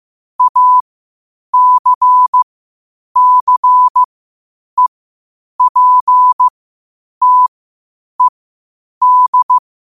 Morse code, heard as a keyed tone, about 15 wpm, spaced as for 8 wpm, 1000 Hz. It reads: ACCEPTED
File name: qotdfiles / 8wpm